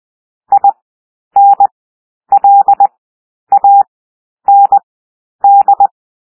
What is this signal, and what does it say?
Morse code, heard as a keyed tone, about 20 wpm, 800 Hz. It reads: I N L A N D
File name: cwmp3 / W1 SOUND WORDS 20 wpm